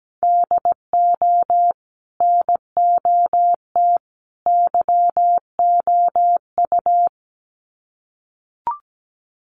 Morse code, heard as a keyed tone, about 17 wpm, 700 Hz. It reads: DO NOT YOU
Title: Sets of 3 Words 17wpm